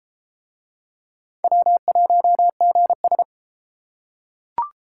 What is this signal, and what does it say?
Morse code, heard as a keyed tone, about 30 wpm, 700 Hz. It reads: W1GS